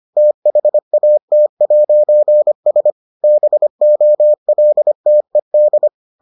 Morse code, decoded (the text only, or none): THAT'S BOLTED